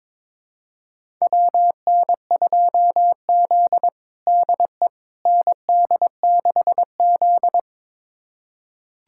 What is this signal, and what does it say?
Morse code, heard as a keyed tone, about 22 wpm, 700 Hz. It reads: WN2Z DE ND6Z
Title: USA Callsign DE Callsign - 4 Characters - 22wpm